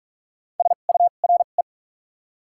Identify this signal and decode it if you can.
Morse code, heard as a keyed tone, about 46 wpm, spaced as for 32 wpm, 700 Hz. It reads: SURE
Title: 2x Character Spacing - Top 300 Words - 45wpm